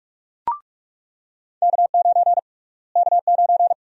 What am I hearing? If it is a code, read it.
Morse code, heard as a keyed tone, about 41 wpm, 700 Hz. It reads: K9 K9
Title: USA Call Sign Prefixes - Repeated 2x - 45wpm